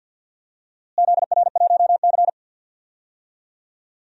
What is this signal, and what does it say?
Morse code, heard as a keyed tone, about 46 wpm, 700 Hz. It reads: ZR1C
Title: DXCC-100 Call Signs - 4 Characters - Repeated 3x - 50wpm